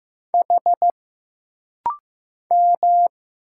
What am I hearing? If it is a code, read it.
Morse code, heard as a keyed tone, about 15 wpm, 700 Hz. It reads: H M
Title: Single Letters - Rapid-Fire 15wpm